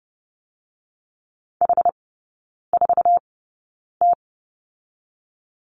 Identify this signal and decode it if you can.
Morse code, heard as a keyed tone, about 30 wpm, spaced as for 9 wpm, 700 Hz. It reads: H4T
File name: Lesson 18e - Letter m - Three Characters 30wpm